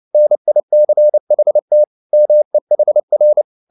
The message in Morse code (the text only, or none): NICHT MEHR